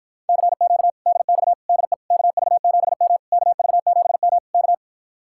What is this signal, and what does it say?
Morse code, heard as a keyed tone, about 53 wpm, 700 Hz. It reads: CQ DX DE KV7M KV7M K